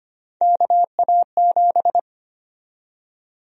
Morse code, heard as a keyed tone, about 25 wpm, 700 Hz. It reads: KA7
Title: USA Call Sign Prefixes - 25wpm